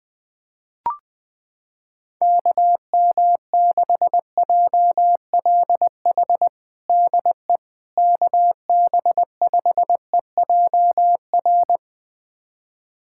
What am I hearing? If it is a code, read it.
Morse code, heard as a keyed tone, about 21 wpm, 700 Hz. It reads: KM6JLH DE KB5EJR
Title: USA Callsign DE Callsign - 6 Characters - 20wpm